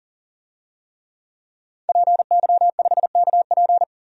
Morse code, heard as a keyed tone, about 37 wpm, 700 Hz. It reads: PY5KP
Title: DXCC-100 Call Signs - 5 Characters - 40wpm